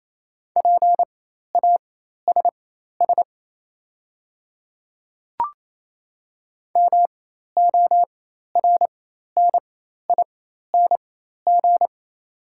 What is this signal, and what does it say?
Morse code, heard as a keyed tone, about 28 wpm, spaced as for 13 wpm, 700 Hz. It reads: PASS MORNING
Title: 4x Character Spacing - Top 500 Words - 28wpm